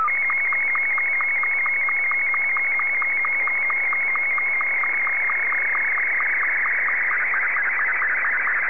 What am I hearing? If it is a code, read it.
SSTV (slow-scan television), Martin 2.
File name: sstv / M2-VK3DFL